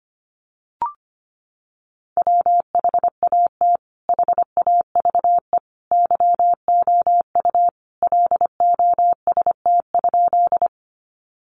Morse code, heard as a keyed tone, about 25 wpm, 700 Hz. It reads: WHAT HAVE YOU LOST?
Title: Sentences from Top 1400 Words 25wpm